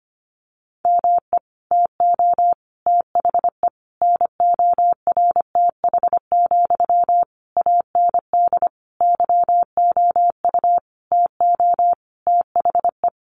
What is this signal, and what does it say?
Morse code, heard as a keyed tone, about 25 wpm, 700 Hz. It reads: ME TO THE NORTH, AND YOU TO THE
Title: Sentences from Top 400 Words 25wpm